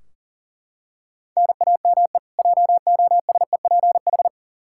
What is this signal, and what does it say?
Morse code, heard as a keyed tone, about 40 wpm, 700 Hz. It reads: NAME JOSEPH
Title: Practice QSO - Exchange 1 - Beginning - 40wpm